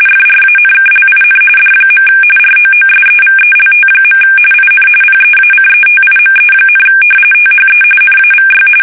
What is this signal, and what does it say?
RTTY, about 45.45 baud. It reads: FRANKLIN D. ROOSEVELT. IN PEACETIME, THE NAVY
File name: fsk850